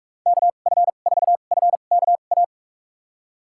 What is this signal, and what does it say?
Morse code, heard as a keyed tone, about 46 wpm, spaced as for 33 wpm, 700 Hz. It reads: KF4FKA
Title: USA Call Signs - 2x Character Spacing - 45wpm